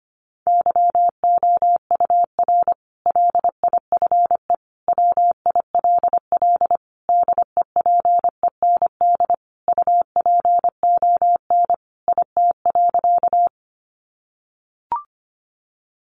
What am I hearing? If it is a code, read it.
Morse code, heard as a keyed tone, about 25 wpm, 700 Hz. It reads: YOUR LIFE WILL DEPEND UPON IT.
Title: Sentences from Top 1300 Words 25wpm